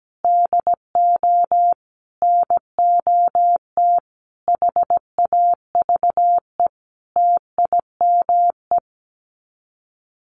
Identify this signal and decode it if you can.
Morse code, heard as a keyed tone, about 17 wpm, 700 Hz. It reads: DO NOT HAVE TIME
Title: Sets of 5 Words Encore 17wpm